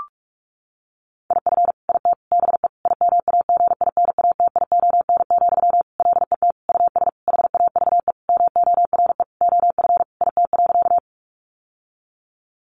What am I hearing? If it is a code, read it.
Morse code, heard as a keyed tone, about 44 wpm, 700 Hz. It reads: IF IT BE IMAGINATION, LET US HAVE MORE OF IT.
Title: Sentences from Top 1300 Words 45wpm